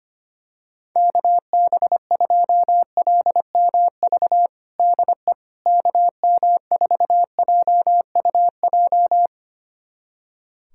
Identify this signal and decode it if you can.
Morse code, heard as a keyed tone, about 25 wpm, 700 Hz. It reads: KB2LMV DE KM4JUJ
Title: USA Callsign DE Callsign - 6 Characters - 25wpm